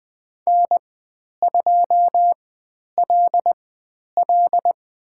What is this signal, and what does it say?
Morse code, heard as a keyed tone, about 20 wpm, 700 Hz. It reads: N 2 L L
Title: 095 Lesson 16g - Letter c - Call Signs 20wpm